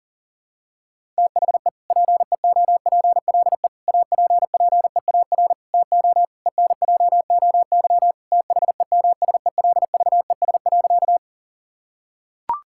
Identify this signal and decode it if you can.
Morse code, heard as a keyed tone, about 40 wpm, 700 Hz. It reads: THE PEOPLE APPEAR TO ENJOY THEMSELVES.
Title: Sentences from Top 1000 Words - Review Part 5 - 40wpm